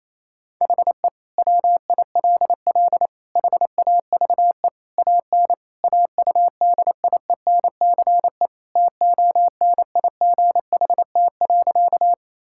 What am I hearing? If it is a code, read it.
Morse code, heard as a keyed tone, about 27 wpm, 700 Hz. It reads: HE WILL HAVE AN AUDIENCE TONIGHT.
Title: Sentences from Top 1000 Words - Review Part 5 - 28wpm